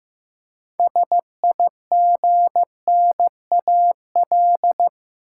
Morse code, heard as a keyed tone, about 15 wpm, 700 Hz. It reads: SIGNAL
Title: Six Letter Words - 15wpm